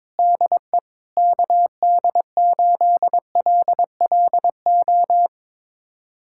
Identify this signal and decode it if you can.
Morse code, heard as a keyed tone, about 22 wpm, 700 Hz. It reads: DE KD8LLO
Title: USA Callsign DE Callsign - 6 Characters - 22wpm